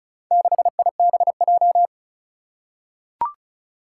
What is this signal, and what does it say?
Morse code, heard as a keyed tone, about 36 wpm, 700 Hz. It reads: 6IBJ
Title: Four Challenging Letter-Number Combinations 35wpm